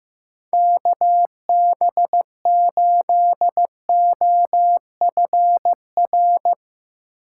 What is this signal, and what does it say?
Morse code, heard as a keyed tone, about 15 wpm, 700 Hz. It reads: KB8OFR
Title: USA Call Signs - 6x Character Spacing - 15wpm